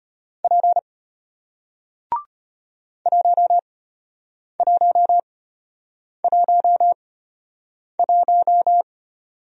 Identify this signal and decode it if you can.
Morse code, heard as a keyed tone, about 33 wpm, spaced as for 8 wpm, 700 Hz. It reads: P 1111